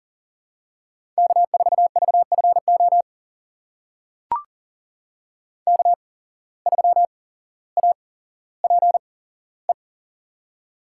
Morse code, heard as a keyed tone, about 40 wpm, spaced as for 17 wpm, 700 Hz. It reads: K4VFO K3APE
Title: USA Call Signs - 5 Characters - 8x Character Spacing - 40wpm